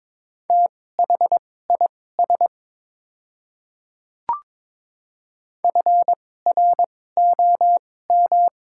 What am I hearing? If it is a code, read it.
Morse code, heard as a keyed tone, about 22 wpm, spaced as for 16 wpm, 700 Hz. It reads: THIS FROM
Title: 2x Character Spacing - Top 200 Words - 22wpm